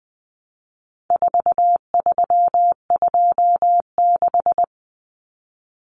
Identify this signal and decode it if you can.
Morse code, heard as a keyed tone, about 20 wpm, 700 Hz. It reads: 4326
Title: Common Cut Numbers - 4-Digit 20wpm